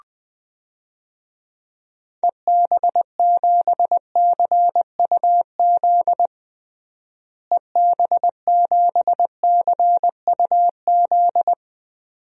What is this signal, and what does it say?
Morse code, heard as a keyed tone, about 21 wpm, 700 Hz. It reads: EB7CUZ EB7CUZ
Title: DXCC-100 Call Signs - 6 Characters - Repeated 2x - 20wpm